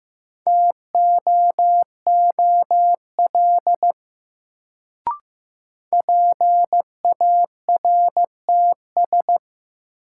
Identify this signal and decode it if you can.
Morse code, heard as a keyed tone, about 15 wpm, 700 Hz. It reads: TOOL PARTS
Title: Sentences from Top 1000 Words - Review Part 6 - New Word Drill - 15wpm